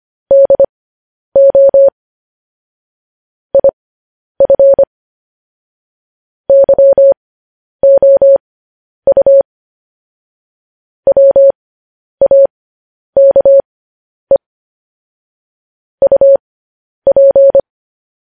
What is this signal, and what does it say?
Morse code, heard as a keyed tone, about 25 wpm, spaced as for 10 wpm, 550 Hz. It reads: DO IF YOU WAKE UP
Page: news headlines in morse code at 10 wpm